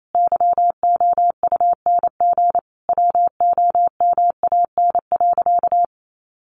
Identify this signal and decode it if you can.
Morse code, heard as a keyed tone, about 28 wpm, 700 Hz. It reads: YOUNG WOMAN.
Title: Sentences from Top 1400 Words 28wpm